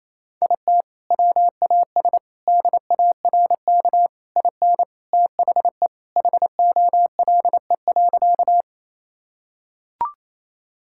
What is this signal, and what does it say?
Morse code, heard as a keyed tone, about 27 wpm, 700 Hz. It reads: IT WAS DARK IN THE HOLE.
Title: Sentences from Top 1600 Words 28wpm